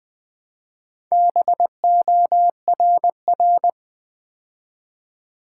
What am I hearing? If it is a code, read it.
Morse code, heard as a keyed tone, about 20 wpm, 700 Hz. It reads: BORR
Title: Build A Word - 2 Letters At A Time - 6-Letter Words - 20wpm